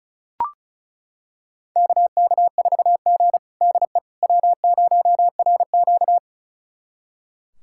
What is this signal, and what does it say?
Morse code, heard as a keyed tone, about 36 wpm, 700 Hz. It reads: KK4G DE W0RQ
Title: USA Callsign DE Callsign - 4 Characters - 35wpm